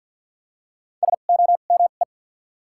Morse code, heard as a keyed tone, about 53 wpm, 700 Hz. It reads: S O M E
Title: 2x Character Spacing - Top 100 Words - 50wpm